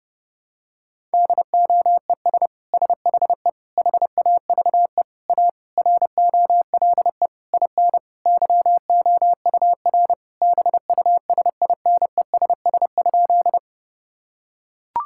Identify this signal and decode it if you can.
Morse code, heard as a keyed tone, about 30 wpm, 700 Hz. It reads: DOES SHE HAVE A ROLE IN YOUR BUSINESS?
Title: Sentences from Top 1000 Words - Review Part 6 - 30wpm